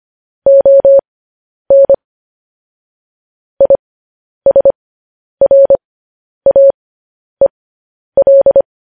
Morse code, heard as a keyed tone, about 25 wpm, spaced as for 10 wpm, 550 Hz. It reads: ON ISRAEL
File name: podcast / News headlines-1x-25x10